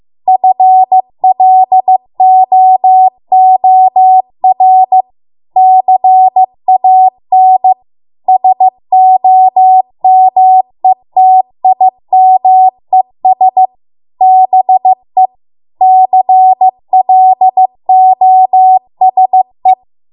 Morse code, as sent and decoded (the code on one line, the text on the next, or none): ..-. .-.. --- --- .-. / -.-. .- -. / ... --- -- . - .. -- . ... / -... / -.-. .-.. --- ... .
FLOOR CAN SOMETIMES BE CLOSE